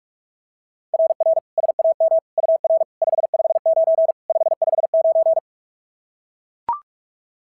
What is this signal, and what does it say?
Morse code, heard as a keyed tone, about 44 wpm, 690 Hz. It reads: RR SAM UR 559 559
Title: SOTA QSO Practice 45wpm